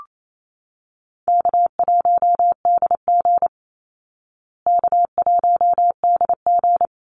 Morse code, heard as a keyed tone, about 28 wpm, 700 Hz. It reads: K1DG K1DG
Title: Easy Call Signs Repeated 2x - 28wpm